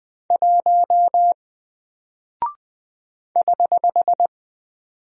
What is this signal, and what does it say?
Morse code, heard as a keyed tone, about 20 wpm, 700 Hz. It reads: 1 <HH>
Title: Single Character (Letter:Number:Punctuation:Prosign) Alt 20wpm